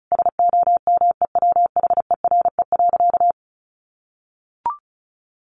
Morse code, heard as a keyed tone, about 34 wpm, 700 Hz. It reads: SOMEWHERE.
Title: Sentences from Top 1400 Words 35wpm